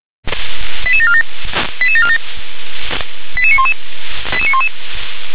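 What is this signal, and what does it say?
Five-tone selective call (ZVEI), heard as quick runs of five tones, tones of about 80 ms. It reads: 88747 88747 88911 88911
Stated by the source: Received by SDR, ZVEI five-tone.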